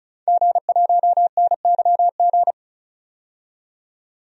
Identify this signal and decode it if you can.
Morse code, heard as a keyed tone, about 34 wpm, 700 Hz. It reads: G1NYG
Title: DXCC-100 Call Signs - 5 Characters - 35wpm